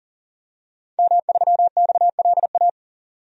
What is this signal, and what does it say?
Morse code, heard as a keyed tone, about 40 wpm, 700 Hz. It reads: M3XLA
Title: DXCC-100 Call Signs - 5 Characters - 40wpm